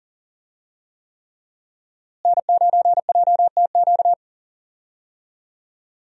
Morse code, heard as a keyed tone, about 40 wpm, 700 Hz. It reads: N9JTQ
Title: USA Call Signs - 5 Characters - Repeated 3x - 40wpm